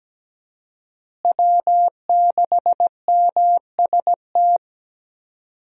Morse code, heard as a keyed tone, about 17 wpm, 700 Hz. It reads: W6MST
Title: USA Call Signs - 5 Characters - Repeated 3x - 17wpm